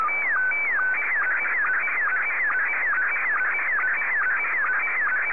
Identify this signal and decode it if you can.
SSTV (slow-scan television), Scottie 1.